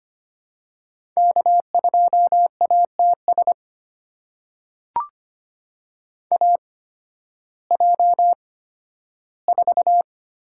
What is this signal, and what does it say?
Morse code, heard as a keyed tone, about 25 wpm, spaced as for 13 wpm, 700 Hz. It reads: K2ATS AJ4
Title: USA Call Signs - 5 Characters - 8x Character Spacing - 25wpm